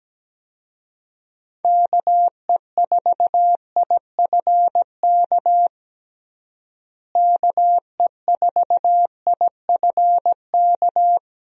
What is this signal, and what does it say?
Morse code, heard as a keyed tone, about 17 wpm, 700 Hz. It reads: KE4IFK KE4IFK
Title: USA Call Signs - 6 Characters - Repeated 3x - 17wpm